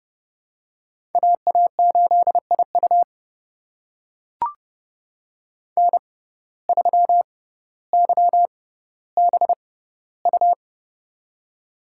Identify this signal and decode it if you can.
Morse code, heard as a keyed tone, about 30 wpm, spaced as for 15 wpm, 700 Hz. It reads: AA8IU N3YBU
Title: USA Call Signs - 5 Characters - 6x Character Spacing - 30wpm